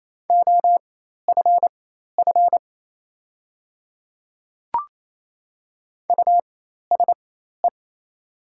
Morse code, heard as a keyed tone, about 27 wpm, spaced as for 13 wpm, 700 Hz. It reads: OFF USE
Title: 4x Character Spacing - Top 200 Words - 28wpm